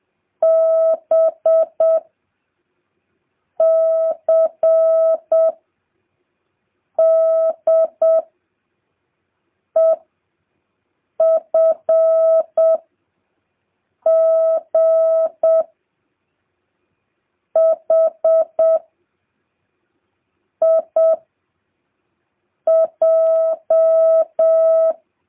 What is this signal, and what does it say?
Morse code, heard as a keyed tone, about 7 wpm, 650 Hz. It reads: B C D E F G H I J